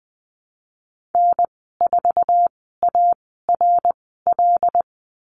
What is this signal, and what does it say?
Morse code, heard as a keyed tone, about 20 wpm, spaced as for 14 wpm, 700 Hz. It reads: N4ARL
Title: 082 Lesson 14g - Number 2 - Call Signs 20wpm